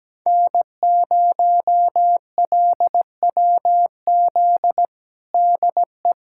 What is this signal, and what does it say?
Morse code, heard as a keyed tone, about 17 wpm, 700 Hz. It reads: N0LWZ DE
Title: USA Callsign DE Callsign - 5 Characters - 17wpm